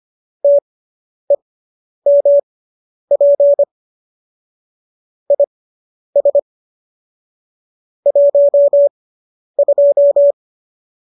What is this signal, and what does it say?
Morse code, heard as a keyed tone, about 25 wpm, spaced as for 10 wpm, 570 Hz. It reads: TEMP IS 12